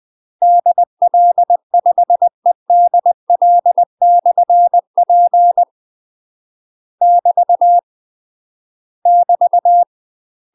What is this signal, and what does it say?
Morse code, heard as a keyed tone, about 20 wpm, 700 Hz. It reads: DL5EDL/P <BT> <BT>
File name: UK MASTER QSOs TRAINING/20WPM